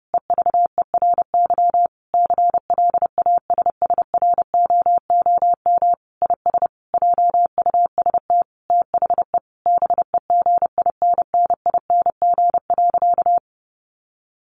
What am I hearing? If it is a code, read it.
Morse code, heard as a keyed tone, about 30 wpm, 700 Hz. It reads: EVERY CLASSROOM IS JUST THE BEGINNING.
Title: Sentences from Top 1500 Words 30wpm